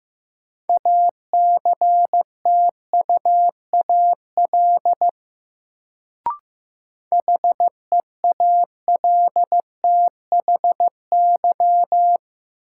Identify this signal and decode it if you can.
Morse code, heard as a keyed tone, about 15 wpm, 700 Hz. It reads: ACTUAL HEALTHY
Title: Sentences from Top 1000 Words - Review Part 5 - New Word Drill - 15wpm